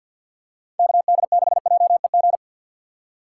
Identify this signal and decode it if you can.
Morse code, heard as a keyed tone, about 49 wpm, 700 Hz. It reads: KD6JEG
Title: USA Call Signs - 3x Character Spacing - 50wpm